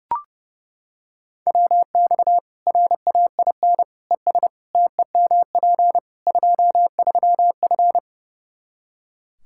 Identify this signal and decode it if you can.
Morse code, heard as a keyed tone, about 32 wpm, 700 Hz. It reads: WX RAIN ES TEMP 23F
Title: K1USN SST QSO Practice 30wpm